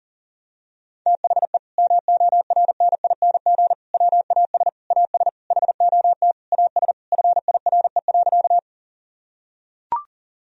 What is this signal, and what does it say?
Morse code, heard as a keyed tone, about 40 wpm, 700 Hz. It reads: THE MORNING WAS AS HOT AS FIRE.
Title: Sentences from Top 1000 Words - Review Part 3 - 40wpm